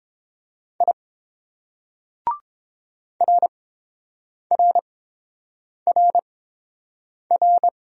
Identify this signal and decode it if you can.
Morse code, heard as a keyed tone, about 26 wpm, spaced as for 7 wpm, 700 Hz. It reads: I RRRR
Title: Speed-Racing Single Letters 22wpm